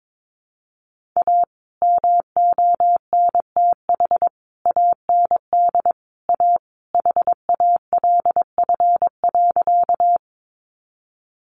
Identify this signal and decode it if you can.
Morse code, heard as a keyed tone, about 22 wpm, 700 Hz. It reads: A MONTH AND A HALF.
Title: Sentences from Top 800 Words 22wpm